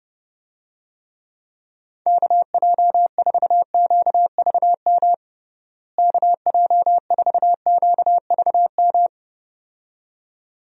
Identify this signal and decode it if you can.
Morse code, heard as a keyed tone, about 32 wpm, 700 Hz. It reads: KJ4QVM KJ4QVM
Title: USA Call Signs - 6 Characters - Repeated 2x - 30wpm